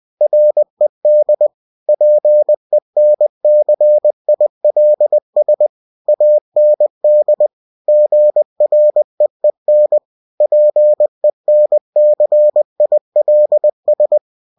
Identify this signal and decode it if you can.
Morse code, heard as a keyed tone, about 20 wpm, 600 Hz. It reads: RED PENCILS AND GREEN PENCILS
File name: cw-books / winnie-the-pooh / winnie-20wpm-page-0109